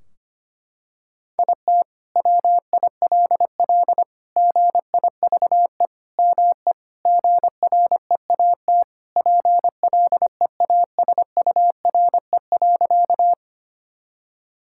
Morse code, heard as a keyed tone, about 25 wpm, 700 Hz. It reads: IT WILL GIVE ME GREAT PLEASURE.
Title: Sentences from Top 1600 Words 25wpm